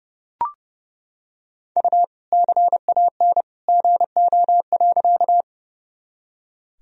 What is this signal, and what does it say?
Morse code, heard as a keyed tone, about 31 wpm, 700 Hz. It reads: U CAN GO.
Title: Sentences from Top 100 Words and Abbreviations 30wpm